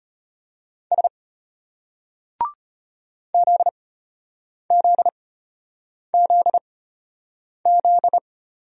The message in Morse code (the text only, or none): S ZZZZ